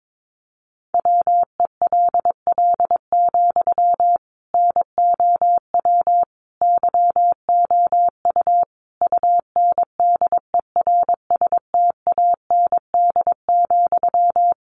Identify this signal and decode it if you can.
Morse code, heard as a keyed tone, about 22 wpm, 700 Hz. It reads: WELL, NOW YOU UNDERSTAND,
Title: Sentences from Top 600 Words 22wpm